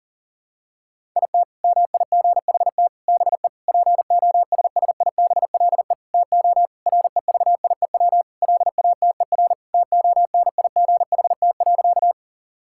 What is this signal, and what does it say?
Morse code, heard as a keyed tone, about 40 wpm, 700 Hz. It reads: IT MIGHT BE POSSIBLE TO REVIEW LATER TONIGHT.
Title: Sentences from Top 1000 Words - Review Part 6 - 40wpm